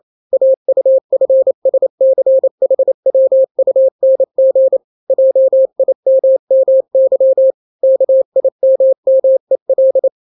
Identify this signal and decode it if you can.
Morse code, heard as a keyed tone, about 28 wpm, 530 Hz. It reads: AUFSCHWUNG JIMMY KIMMEL
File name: daily / 27wpm